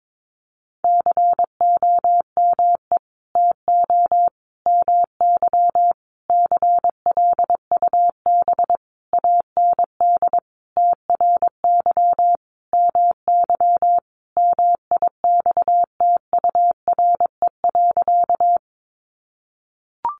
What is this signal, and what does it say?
Morse code, heard as a keyed tone, about 22 wpm, 700 Hz. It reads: COME TO MY CLUB AND TRY MY MIXTURE.
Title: Sentences from Top 1500 Words 22wpm